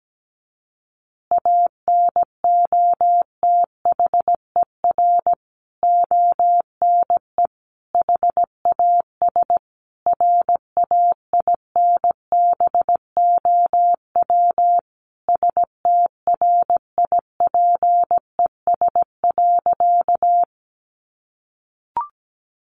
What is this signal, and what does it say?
Morse code, heard as a keyed tone, about 17 wpm, 700 Hz. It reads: ANOTHER ONE HAS RAINBOW STRIPES.
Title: Additional Easy Phrases 17wpm